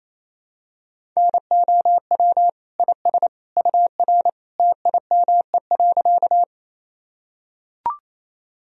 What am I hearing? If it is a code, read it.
Morse code, heard as a keyed tone, about 28 wpm, 700 Hz. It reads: NOW IS UR TIME.
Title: Sentences from Top 100 Words and Abbreviations 28wpm